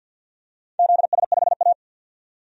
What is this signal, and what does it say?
Morse code, heard as a keyed tone, about 50 wpm, 700 Hz. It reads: ZS5A
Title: DXCC-100 Call Signs - 4 Characters - 50wpm